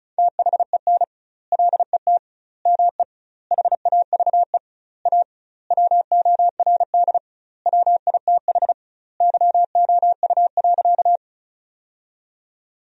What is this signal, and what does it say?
Morse code, heard as a keyed tone, about 36 wpm, 700 Hz. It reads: THEN LET ME HAVE A WORD WITH YOU.
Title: 2x Word-Spacing - Sentences from Top 300 Words - 35wpm